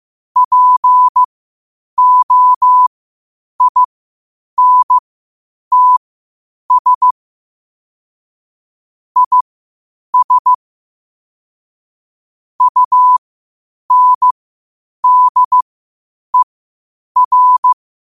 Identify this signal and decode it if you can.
Morse code, heard as a keyed tone, about 15 wpm, spaced as for 8 wpm, 1000 Hz. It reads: POINTS IS UNDER
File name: qotdfiles / 8wpm